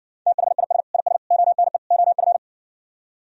Morse code, heard as a keyed tone, about 60 wpm, 700 Hz. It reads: THIS IS ONE OF